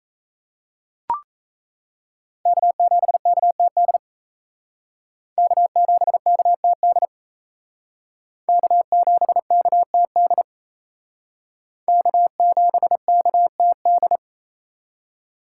Morse code, heard as a keyed tone, about 36 wpm, 700 Hz. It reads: K7KTD K7KTD K7KTD K7KTD